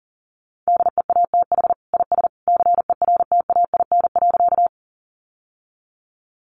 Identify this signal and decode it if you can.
Morse code, heard as a keyed tone, about 40 wpm, 700 Hz. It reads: DEATH IS CERTAIN.
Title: Sentences from Top 1100 Words 40wpm